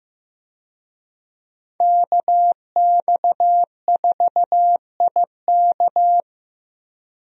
Morse code, heard as a keyed tone, about 15 wpm, 700 Hz. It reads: KX4IK
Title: USA Call Signs - 5 Characters - 15wpm